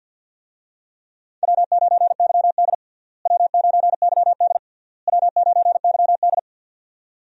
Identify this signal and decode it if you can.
Morse code, heard as a keyed tone, about 49 wpm, 700 Hz. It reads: W9YD W9YD W9YD
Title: USA Call Signs - 4 Characters - Repeated 3x - 50wpm